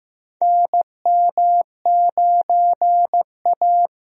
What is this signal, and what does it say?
Morse code, heard as a keyed tone, about 15 wpm, 700 Hz. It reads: NM9A
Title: USA Call Signs - 4 Characters - Repeated 3x - 15wpm